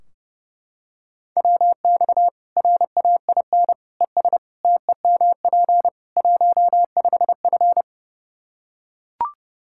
Morse code, heard as a keyed tone, about 30 wpm, 700 Hz. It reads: WX RAIN ES TEMP 15F
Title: Practice QSO - Exchange 2 - Info - 30wpm